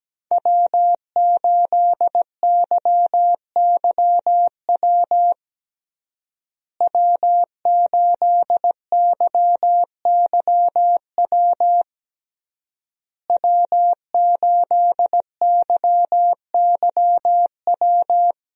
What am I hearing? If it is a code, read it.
Morse code, heard as a keyed tone, about 17 wpm, 700 Hz. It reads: W8YYW W8YYW W8YYW
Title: USA Call Signs - 5 Characters - Repeated 3x - 17wpm